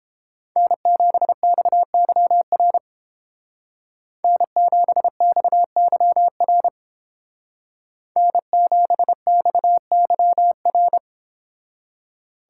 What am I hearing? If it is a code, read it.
Morse code, heard as a keyed tone, about 29 wpm, 700 Hz. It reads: N7XYR N7XYR N7XYR